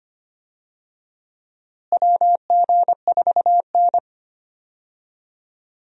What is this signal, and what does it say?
Morse code, heard as a keyed tone, about 25 wpm, 700 Hz. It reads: WG4N
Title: USA Call Signs - 4 Characters - Repeated 3x - 25wpm